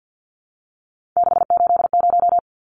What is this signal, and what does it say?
Morse code, heard as a keyed tone, about 50 wpm, 700 Hz. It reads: 680
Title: Numbers - 3-Digit 50wpm